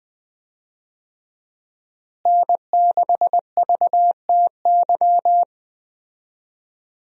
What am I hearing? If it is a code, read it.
Morse code, heard as a keyed tone, about 21 wpm, 700 Hz. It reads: N6VTY
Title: USA Call Signs - 5 Characters - Repeated 3x - 20wpm